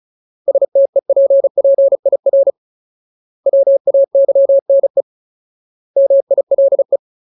Morse code, heard as a keyed tone, about 34 wpm, 550 Hz. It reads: STEPPIR WAYNE MILE
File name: Ultimate Word List / 35WPM Ultimate